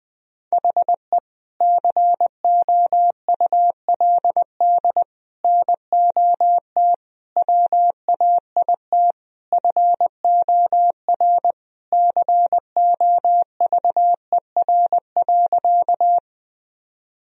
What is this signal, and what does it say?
Morse code, heard as a keyed tone, about 20 wpm, 700 Hz. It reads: HE COULD NOT WAIT FOR COVER.
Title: Sentences from Top 1000 Words - Review Part 1 - 20wpm